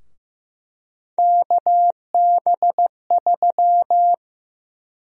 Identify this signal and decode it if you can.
Morse code, heard as a keyed tone, about 15 wpm, 700 Hz. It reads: KB3
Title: USA Call Sign Prefixes - Repeated 2x - 15wpm